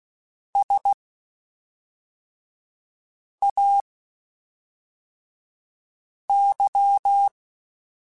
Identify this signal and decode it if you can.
Morse code, heard as a keyed tone, about 16 wpm, spaced as for 3 wpm, 800 Hz. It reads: SAY
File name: (spaced) / QSOs Spaced-mp3s / kc7sac @5 sp